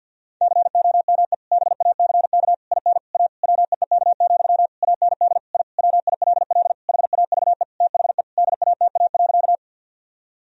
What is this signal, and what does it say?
Morse code, heard as a keyed tone, about 52 wpm, 700 Hz. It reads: COME BACK IN A WEEK, AND I WILL HAVE THE DATA.
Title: Sentences from Top 1000 Words - Review Part 3 - 50wpm